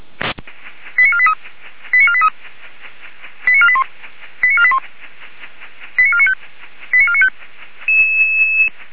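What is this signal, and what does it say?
Five-tone selective call (ZVEI), heard as quick runs of five tones, tones of about 70 ms. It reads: 89483 89483 89481 89481 89495 89495
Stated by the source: Five-tone ZVEI selective call received by SDR, logged by Feuerwehr Orlamünde.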